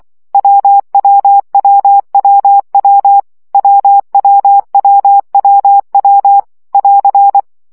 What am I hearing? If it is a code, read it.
Morse code, heard as a keyed tone, about 24 wpm, 800 Hz. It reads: WWWWW WWWWW <AR>
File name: CW gyakorlas w 20 baud